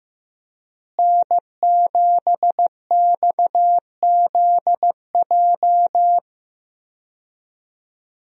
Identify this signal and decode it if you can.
Morse code, heard as a keyed tone, about 15 wpm, 700 Hz. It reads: N7XZJ
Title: USA Call Signs - 5 Characters - Repeated 3x - 15wpm